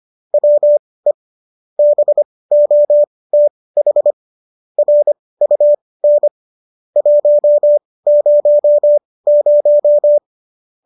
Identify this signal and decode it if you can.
Morse code, heard as a keyed tone, about 25 wpm, spaced as for 18 wpm, 590 Hz. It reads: WE BOTH RUN 100